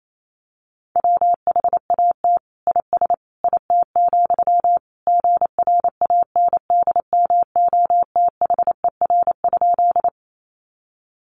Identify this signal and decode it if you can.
Morse code, heard as a keyed tone, about 28 wpm, 700 Hz. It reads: WHAT IS IT, GRANDMOTHER?
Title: Sentences from Top 1200 Words 28wpm